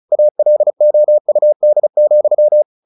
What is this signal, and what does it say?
Morse code, heard as a keyed tone, about 34 wpm, 600 Hz. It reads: ALOUD,
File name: the old man and the sea / the old man and the sea 1 35wpm